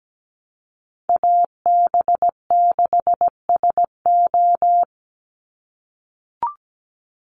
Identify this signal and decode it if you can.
Morse code, heard as a keyed tone, about 17 wpm, 700 Hz. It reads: AB6SO
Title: NA SOTA Chaser Callsigns 17wpm